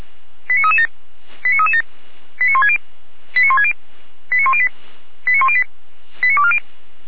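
Five-tone selective call (ZVEI), heard as quick runs of five tones, tones of about 70 ms. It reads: 89337 89337 89160 89160 89108 89108 89240
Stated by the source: ZVEI five-tone selective call received on an SDR.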